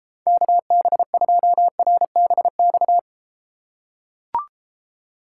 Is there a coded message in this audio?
Yes: Morse code from 0.3 to 3.0 s, heard as a keyed tone, about 33 wpm, 700 Hz. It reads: KB2RBX